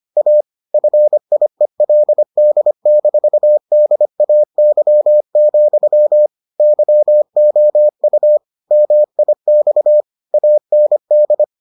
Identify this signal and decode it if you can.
Morse code, heard as a keyed tone, about 25 wpm, 600 Hz. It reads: A FIELD-DAY, YOU MIX AND